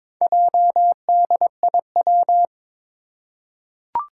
Morse code, heard as a keyed tone, about 22 wpm, 700 Hz. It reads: JDIW
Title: Four Challenging Letter-Number Combinations 22wpm